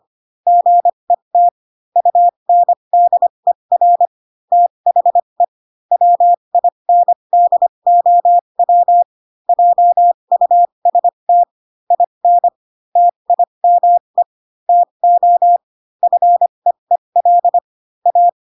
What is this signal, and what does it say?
Morse code, heard as a keyed tone, about 25 wpm, spaced as for 22 wpm, 700 Hz. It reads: GET UNDER THE WINDOW JUST IN TIME TO FEEL A
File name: Pinocchio / Pinocchio ch6 18 25 ik1hge